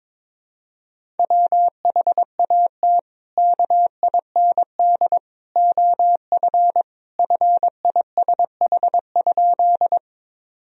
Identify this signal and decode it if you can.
Morse code, heard as a keyed tone, about 22 wpm, 700 Hz. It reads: WHAT KIND OF FISH?
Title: Sentences from Top 1000 Words - Review Part 1 - 22wpm